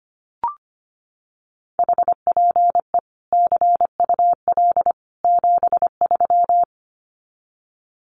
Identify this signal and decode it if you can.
Morse code, heard as a keyed tone, about 25 wpm, 700 Hz. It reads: HPE CUL 73
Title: Common QSO Phrases 25wpm